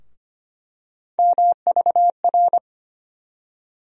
Morse code, heard as a keyed tone, about 25 wpm, 700 Hz. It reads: MVR
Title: USA Call Sign Postfixes - 3 Letters 25wpm